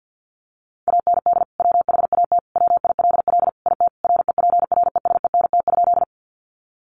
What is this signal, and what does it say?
Morse code, heard as a keyed tone, about 48 wpm, 700 Hz. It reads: AND WHAT WILL IT REPRESENT?
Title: Sentences from Top 1200 Words 50wpm